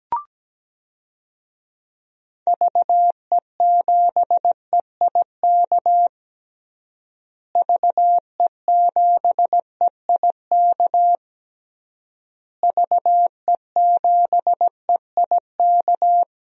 Morse code, heard as a keyed tone, about 17 wpm, 700 Hz. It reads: VE7EIK VE7EIK VE7EIK
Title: DXCC-100 Call Signs - 6 Characters - Repeated 3x - 17wpm